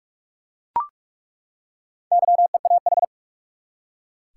Morse code, heard as a keyed tone, about 42 wpm, 700 Hz. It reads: YEAH
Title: Four Letter Words - 45wpm